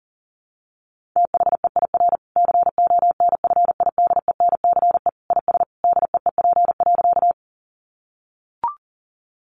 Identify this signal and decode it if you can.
Morse code, heard as a keyed tone, about 40 wpm, 700 Hz. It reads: THEIR CONFIDENCE IS DEEP.
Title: Sentences from Top 1500 Words 40wpm